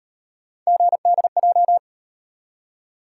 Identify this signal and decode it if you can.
Morse code, heard as a keyed tone, about 32 wpm, 700 Hz. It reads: GDJ